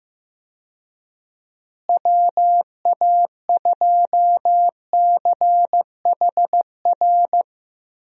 Morse code, heard as a keyed tone, about 15 wpm, 700 Hz. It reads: WA2CHR
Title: USA Call Signs - 6 Characters - Repeated 2x - 15wpm